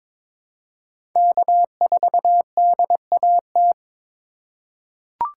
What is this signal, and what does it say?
Morse code, heard as a keyed tone, about 22 wpm, 700 Hz. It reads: K4DAT
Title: USA Call Signs - 4x Character Spacing - 22wpm